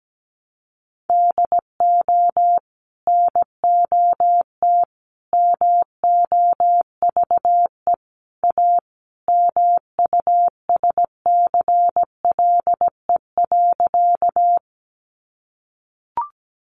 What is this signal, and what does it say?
Morse code, heard as a keyed tone, about 17 wpm, 700 Hz. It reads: DO NOT MOVE A MUSCLE.
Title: Sentences from Top 1300 Words 17wpm